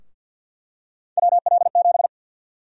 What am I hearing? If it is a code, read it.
Morse code, heard as a keyed tone, about 51 wpm, 700 Hz. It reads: WL7
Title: USA Call Sign Prefixes - Repeated 2x - 50wpm